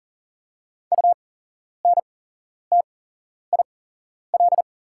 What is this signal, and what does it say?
Morse code, heard as a keyed tone, about 40 wpm, spaced as for 11 wpm, 700 Hz. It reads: UNTIL
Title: 8x Character Spacing - Top 500 Words - 40wpm